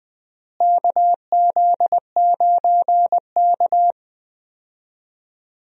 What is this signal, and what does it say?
Morse code, heard as a keyed tone, about 20 wpm, 700 Hz. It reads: KZ9K
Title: USA Call Signs - 4 Characters - Repeated 3x - 20wpm